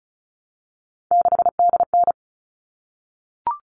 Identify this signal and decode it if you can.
Morse code, heard as a keyed tone, about 34 wpm, 700 Hz. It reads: 6DN
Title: All Cut Numbers - 3-Digit 35wpm